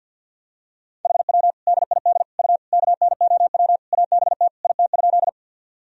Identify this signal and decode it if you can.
Morse code, heard as a keyed tone, about 51 wpm, 700 Hz. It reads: HW DID U KNOW ABT IT?
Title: Sentences from Top 100 Words and Abbreviations 50wpm